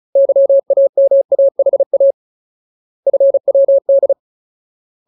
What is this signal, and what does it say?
Morse code, heard as a keyed tone, about 34 wpm, 550 Hz. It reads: YAMAHA FWD
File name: Ultimate Word List / 35WPM Ultimate